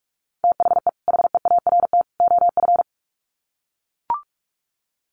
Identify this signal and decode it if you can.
Morse code, heard as a keyed tone, about 44 wpm, 700 Hz. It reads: THE HEART OF
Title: Sets of 3 Words 45wpm